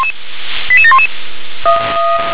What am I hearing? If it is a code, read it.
Five-tone selective call (ZVEI), heard as quick runs of five tones, tones of about 70 ms. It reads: 88711